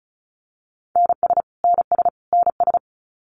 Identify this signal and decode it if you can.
Morse code, heard as a keyed tone, about 34 wpm, 700 Hz. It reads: NS NS NS
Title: Canadian Provinces and Territories - Repeated 3x - 35wpm